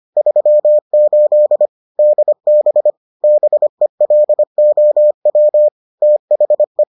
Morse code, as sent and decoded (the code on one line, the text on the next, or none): ...-- ---.. / -.. -... / -... .-.. --- .-- / - .... .
38 DB BELOW THE